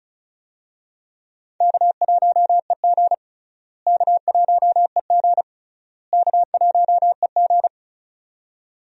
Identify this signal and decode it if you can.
Morse code, heard as a keyed tone, about 34 wpm, 700 Hz. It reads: K1EG K1EG K1EG
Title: USA Call Signs - 4 Characters - Repeated 3x - 35wpm